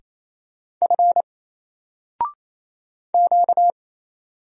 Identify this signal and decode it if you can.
Morse code, heard as a keyed tone, about 27 wpm, 700 Hz. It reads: F Q
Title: Single Letter-Number Alt 28wpm